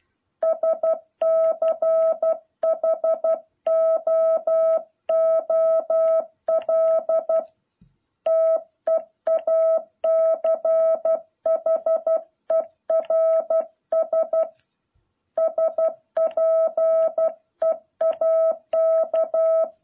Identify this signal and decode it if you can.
Morse code, heard as a keyed tone, about 12 wpm, 650 Hz. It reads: SCHOOL TEACHERS SPEAK
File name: jh8cbh / 2021cw